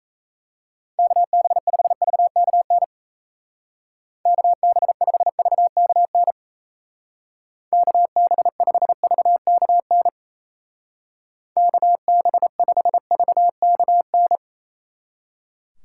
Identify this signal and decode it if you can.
Morse code, heard as a keyed tone, about 34 wpm, 700 Hz. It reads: KB5VKN KB5VKN KB5VKN KB5VKN
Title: Speed Racing - USA Call Signs 28wpm